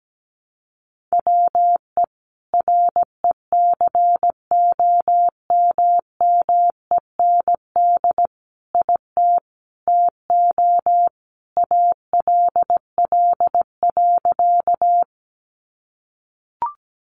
Morse code, heard as a keyed tone, about 17 wpm, 700 Hz. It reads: WE RECOMMEND IT TO ALL.
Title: Sentences from Top 1200 Words 17wpm